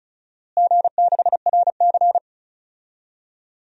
Morse code, heard as a keyed tone, about 36 wpm, 690 Hz. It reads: G6RC
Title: DXCC-100 Call Signs - 4 Characters - Repeated 2x - 35wpm